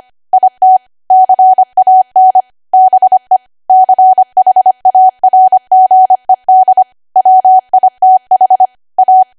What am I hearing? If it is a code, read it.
Morse code, heard as a keyed tone, about 25 wpm, 750 Hz. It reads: IT CAN BE CHARGED WITH A
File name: cwa / W1AW-Level3 / 25WPM